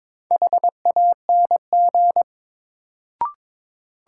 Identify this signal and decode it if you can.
Morse code, heard as a keyed tone, about 22 wpm, 700 Hz. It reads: HANG
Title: Sentences from Top 1600 Words - New Word Drill - 22wpm